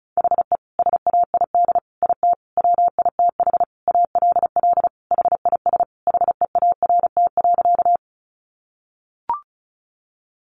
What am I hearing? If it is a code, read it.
Morse code, heard as a keyed tone, about 34 wpm, 700 Hz. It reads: HE SAID IT WITH ALL HIS HEART.
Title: Sentences from Top 1200 Words 35wpm